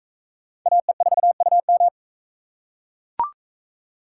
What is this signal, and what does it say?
Morse code, heard as a keyed tone, about 42 wpm, 700 Hz. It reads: AE4UM